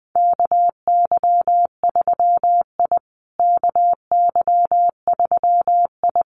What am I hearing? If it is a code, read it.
Morse code, heard as a keyed tone, about 20 wpm, 700 Hz. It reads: KY3I KY3I
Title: USA Call Signs Repeated 2x 20wpm